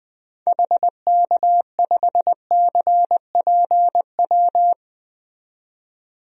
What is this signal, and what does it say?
Morse code, heard as a keyed tone, about 20 wpm, 700 Hz. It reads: HK5CPW
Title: DXCC-100 Call Signs - 6 Characters - 20wpm